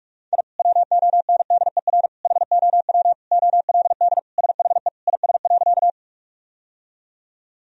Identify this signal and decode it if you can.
Morse code, heard as a keyed tone, about 46 wpm, 700 Hz. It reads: I WONDER HOW OLD SHE IS.
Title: Sentences from Top 1000 Words - Review Part 3 - 45wpm